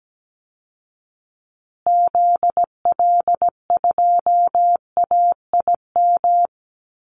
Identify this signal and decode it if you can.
Morse code, heard as a keyed tone, about 17 wpm, 700 Hz. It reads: ZL2AIM
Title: Call Signs Repeated 3x - 17wpm